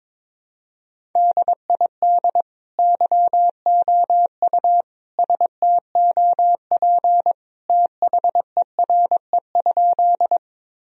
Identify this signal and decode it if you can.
Morse code, heard as a keyed tone, about 22 wpm, 700 Hz. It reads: DID YOU STOP THERE?
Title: Sentences from Top 1000 Words - Review Part 1 - 22wpm